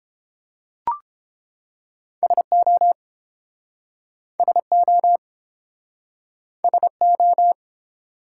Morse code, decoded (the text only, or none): SO SO SO